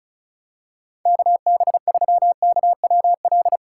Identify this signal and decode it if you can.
Morse code, heard as a keyed tone, about 36 wpm, 700 Hz. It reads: KB3KWL
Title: USA Call Signs - 6 Characters - 2x Character Spacing - 35wpm